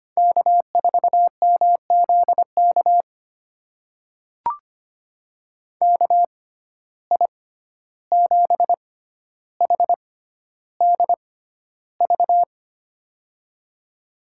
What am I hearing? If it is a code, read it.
Morse code, heard as a keyed tone, about 25 wpm, spaced as for 12 wpm, 700 Hz. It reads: K4MZK KI7HDV
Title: USA Call Signs - 6x Character Spacing - 25wpm